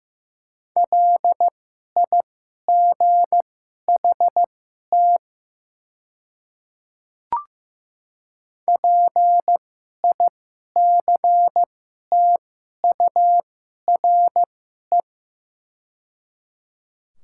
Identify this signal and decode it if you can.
Morse code, heard as a keyed tone, about 15 wpm, spaced as for 11 wpm, 690 Hz. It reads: LIGHT PICTURE
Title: 2x Character Spacing - Top 300 Words - 15wpm